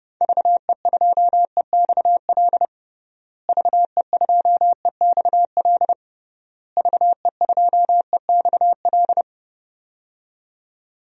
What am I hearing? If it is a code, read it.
Morse code, heard as a keyed tone, about 30 wpm, 700 Hz. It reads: VE2EXL VE2EXL VE2EXL
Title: DXCC-100 Call Signs - 6 Characters - Repeated 3x - 30wpm